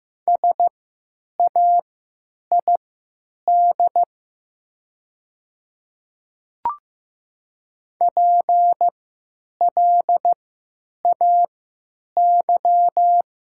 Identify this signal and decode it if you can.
Morse code, heard as a keyed tone, about 15 wpm, spaced as for 9 wpm, 700 Hz. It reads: SAID PLAY
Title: 3x Character Spacing - Top 200 Words - 15wpm